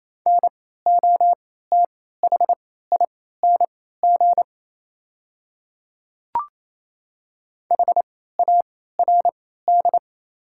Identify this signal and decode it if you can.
Morse code, heard as a keyed tone, about 27 wpm, spaced as for 16 wpm, 700 Hz. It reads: NOTHING HARD
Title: 3x Character Spacing - Top 500 Words - 28wpm